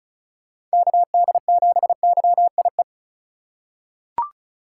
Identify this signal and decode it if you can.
Morse code, heard as a keyed tone, about 36 wpm, 700 Hz. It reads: KD7YIE
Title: USA Call Signs - 3x Character Spacing - 35wpm